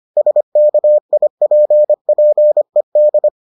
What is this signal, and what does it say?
Morse code, heard as a keyed tone, about 25 wpm, 600 Hz. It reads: SKIPPED